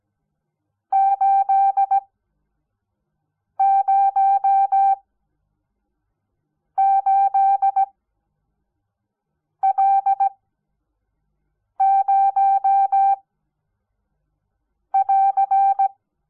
Morse code, heard as a keyed tone, about 17 wpm, spaced as for 5 wpm, 780 Hz. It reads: 808L0 <AR>
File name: Onlinelehrgang / morsen / DJ4UF Morsen06